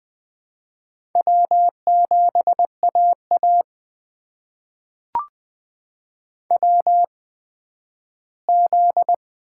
Morse code, heard as a keyed tone, about 20 wpm, spaced as for 12 wpm, 700 Hz. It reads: W7AA WZ
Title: USA Call Signs - 4 Characters - 8x Character Spacing - 20wpm